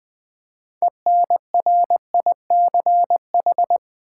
Morse code, heard as a keyed tone, about 20 wpm, 700 Hz. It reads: ENRICH
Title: Build A Word - 2 Letters At A Time - 6-Letter Words - 20wpm